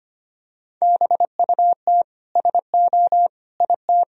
Morse code, heard as a keyed tone, about 25 wpm, 690 Hz. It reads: BUT SO IT
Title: Sentences from Top 1000 Words - Review Part 4 - 25wpm